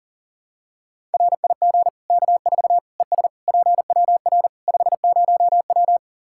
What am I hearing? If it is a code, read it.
Morse code, heard as a keyed tone, about 40 wpm, 700 Hz. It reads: RIG K4 ES PWR 50W
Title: Practice QSO - Exchange 2 - Info - 40wpm